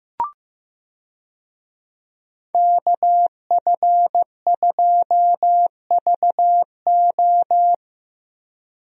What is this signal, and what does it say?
Morse code, heard as a keyed tone, about 15 wpm, 700 Hz. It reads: KF2VO
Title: USA Call Signs - 5 Characters - Repeated 3x - 15wpm